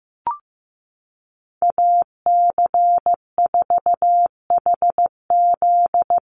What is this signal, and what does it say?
Morse code, heard as a keyed tone, about 15 wpm, 700 Hz. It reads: AC4HZ
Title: USA Call Signs 15wpm